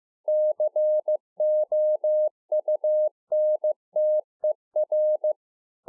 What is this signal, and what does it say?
Morse code, heard as a keyed tone, about 15 wpm, 610 Hz. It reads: COUNTER
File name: 2011-04-mp3 / A SMALL TRANCEIVER 2 15WPM